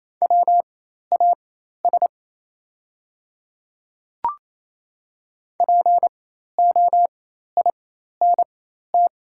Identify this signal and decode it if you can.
Morse code, heard as a keyed tone, about 27 wpm, spaced as for 13 wpm, 690 Hz. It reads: WAS POINT
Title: 4x Character Spacing - Top 200 Words - 28wpm